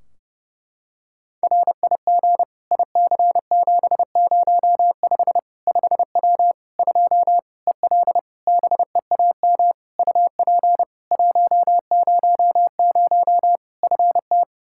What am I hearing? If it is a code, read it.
Morse code, heard as a keyed tone, about 30 wpm, 700 Hz. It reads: RIG IC705 5W 2 EL BEAM UP 100 FT
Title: K1USN SST QSO Practice 30wpm